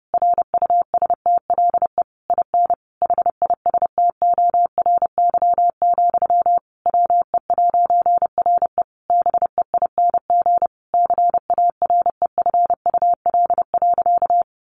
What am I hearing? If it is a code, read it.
Morse code, heard as a keyed tone, about 30 wpm, 700 Hz. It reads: RUSTLE IN HISTORY, WE'RE BEING CAREFUL.
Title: A Sound Of Thunder 30wpm